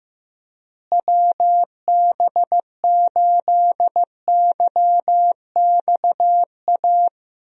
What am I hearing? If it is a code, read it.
Morse code, heard as a keyed tone, about 16 wpm, 700 Hz. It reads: WB8YXA
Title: USA Call Signs - 6 Characters - Repeated 2x - 15wpm